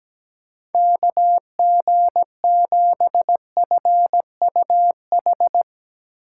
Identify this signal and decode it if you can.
Morse code, heard as a keyed tone, about 17 wpm, 700 Hz. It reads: KG7FUH
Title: USA Call Signs - 6 Characters - 8x Character Spacing - 17wpm